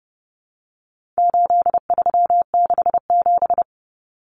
Morse code, heard as a keyed tone, about 30 wpm, 700 Hz. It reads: 8367
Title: Common Cut Numbers - 4-Digit 30wpm